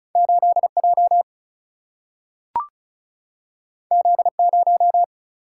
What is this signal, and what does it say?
Morse code, heard as a keyed tone, about 36 wpm, 700 Hz. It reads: 8J Z0
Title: Single-Letter Single-Number 35wpm